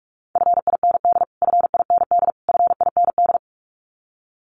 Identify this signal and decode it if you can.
Morse code, heard as a keyed tone, about 44 wpm, 700 Hz. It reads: FIND FIND FIND
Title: Top 500 Words Repeated 45wpm